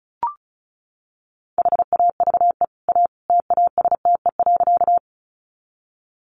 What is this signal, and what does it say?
Morse code, heard as a keyed tone, about 34 wpm, 700 Hz. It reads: HAVE A TASTE.
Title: Sentences from Top 1100 Words 35wpm